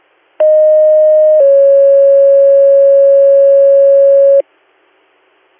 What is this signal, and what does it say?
Two-tone page, heard as a one-second tone, then a three-second tone, about 620 then 550 Hz. It is heard from 0.4 to 4.4 s.